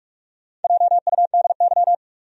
Morse code, heard as a keyed tone, about 46 wpm, 700 Hz. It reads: JUDY
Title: USA Common Names - Max 4 Letters - 45wpm